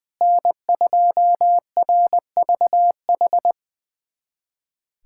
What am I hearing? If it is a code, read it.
Morse code, heard as a keyed tone, about 20 wpm, 700 Hz. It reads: N2RVH
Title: USA Callsign DE Callsign - 5 Characters - 20wpm